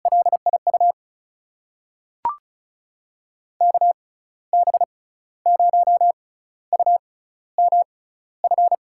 Morse code, heard as a keyed tone, about 34 wpm, spaced as for 15 wpm, 700 Hz. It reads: LIU KB0UMF
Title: USA Call Signs - 6 Characters - 6x Character Spacing - 35wpm